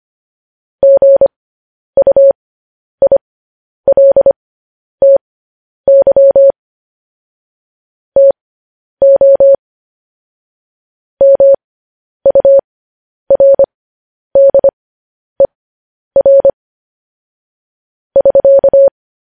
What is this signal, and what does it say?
Morse code, heard as a keyed tone, about 25 wpm, spaced as for 10 wpm, 550 Hz. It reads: GUILTY TO MURDER <SK>